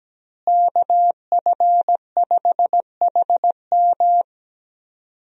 Morse code, heard as a keyed tone, about 17 wpm, 700 Hz. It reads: KF5HM
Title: USA Call Signs - 5 Characters - Repeated 3x - 17wpm